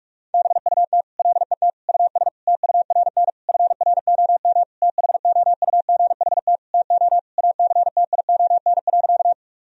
Morse code, heard as a keyed tone, about 46 wpm, 700 Hz. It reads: BUT LET US TURN FROM THOUGHT TO ACTION.
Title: Sentences from Top 1000 Words - Review Part 4 - 45wpm